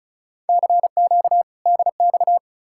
Morse code, heard as a keyed tone, about 47 wpm, spaced as for 41 wpm, 700 Hz. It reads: CQ DX
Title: Speed-Racing Common QSO Phrases - 35wpm